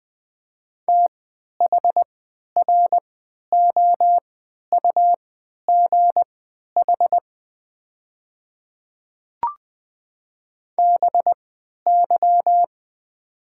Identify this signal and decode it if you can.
Morse code, heard as a keyed tone, about 20 wpm, spaced as for 11 wpm, 690 Hz. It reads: THROUGH BY
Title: 3x Character Spacing - Top 300 Words - 20wpm